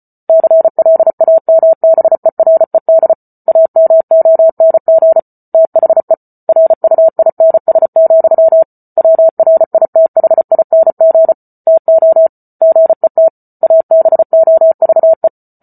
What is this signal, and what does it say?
Morse code, heard as a keyed tone, about 34 wpm, 650 Hz. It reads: CLAMBERED AMONG THE RUINS, WRITHING TO GET ABOVE